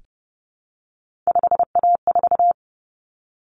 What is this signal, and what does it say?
Morse code, heard as a keyed tone, about 30 wpm, 700 Hz. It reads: 5A4
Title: Common Cut Numbers - 3-Digit 30wpm